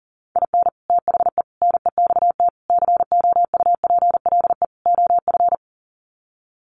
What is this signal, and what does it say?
Morse code, heard as a keyed tone, about 40 wpm, 700 Hz. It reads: IN THE NEXT COUPLE OF
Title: Sets of 5 Words Encore2 40wpm